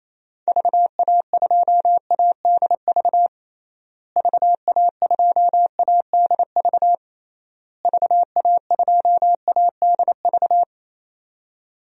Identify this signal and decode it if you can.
Morse code, heard as a keyed tone, about 28 wpm, 700 Hz. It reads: VA2ADV VA2ADV VA2ADV
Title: DXCC-100 Call Signs - 6 Characters - Repeated 3x - 28wpm